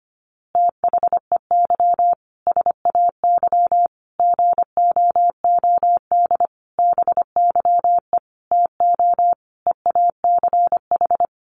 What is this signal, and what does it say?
Morse code, heard as a keyed tone, about 25 wpm, 700 Hz. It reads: THEY SAY GOOD BYE TO EACH
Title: Easy Phrases 25wpm